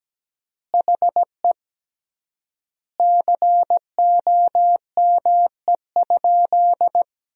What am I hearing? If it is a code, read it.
Morse code, heard as a keyed tone, about 17 wpm, 700 Hz. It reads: HE COME?
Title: 3x Word-Spacing - Sentences from Top 200 Words - 17wpm